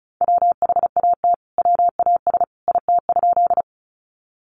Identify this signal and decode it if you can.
Morse code, heard as a keyed tone, about 34 wpm, 700 Hz. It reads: WHAT WAS IT?
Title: Sentences from Top 100 Words 35wpm